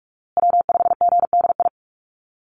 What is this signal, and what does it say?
Morse code, heard as a keyed tone, about 44 wpm, 700 Hz. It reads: W5GDI
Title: USA Call Signs Repeated 2x 45wpm